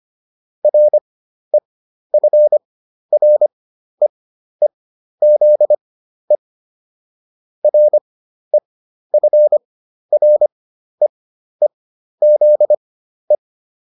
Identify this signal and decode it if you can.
Morse code, heard as a keyed tone, about 25 wpm, spaced as for 12 wpm, 600 Hz. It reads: REFREEZE REFREEZE